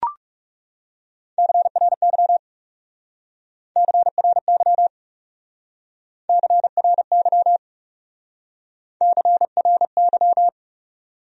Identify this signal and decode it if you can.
Morse code, heard as a keyed tone, about 36 wpm, 700 Hz. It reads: CRY CRY CRY CRY
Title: Speed-Racing Top 500 Words 30wpm